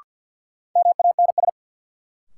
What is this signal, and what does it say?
Morse code, heard as a keyed tone, about 53 wpm, 700 Hz. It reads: MANS
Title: Sentences from Top 1000 Words - Review Part 6 - New Word Drill - 50wpm